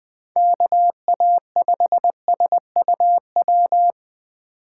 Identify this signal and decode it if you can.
Morse code, heard as a keyed tone, about 20 wpm, 700 Hz. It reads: KA5SUW
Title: USA Call Signs - 3x Character Spacing - 20wpm